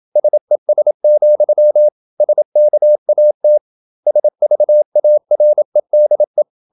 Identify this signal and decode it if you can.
Morse code, heard as a keyed tone, about 27 wpm, 600 Hz. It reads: SES, SKAT SVAREDE
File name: Ferie QRQ 27wpm